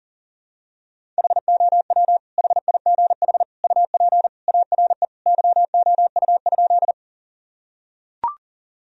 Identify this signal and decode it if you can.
Morse code, heard as a keyed tone, about 40 wpm, 700 Hz. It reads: HOW HIGH UP ARE YOU?
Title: Sentences from Top 1000 Words - Review Part 1 - 40wpm